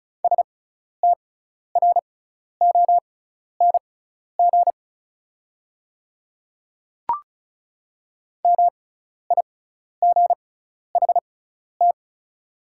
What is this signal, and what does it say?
Morse code, heard as a keyed tone, about 36 wpm, spaced as for 12 wpm, 700 Hz. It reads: STRONG MIGHT
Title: 6x Character Spacing - Top 500 Words - 35wpm